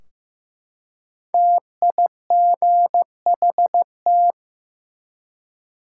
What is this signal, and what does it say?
Morse code, heard as a keyed tone, about 15 wpm, 700 Hz. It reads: TIGHT
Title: Five Letter Words - 15wpm